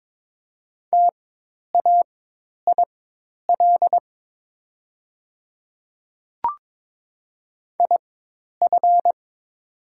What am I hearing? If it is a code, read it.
Morse code, heard as a keyed tone, about 22 wpm, spaced as for 10 wpm, 700 Hz. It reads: TAIL IF
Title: 4x Character Spacing - Top 500 Words - 22wpm